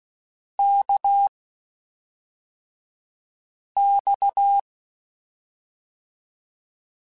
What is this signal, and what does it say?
Morse code, heard as a keyed tone, about 16 wpm, spaced as for 3 wpm, 800 Hz. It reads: KX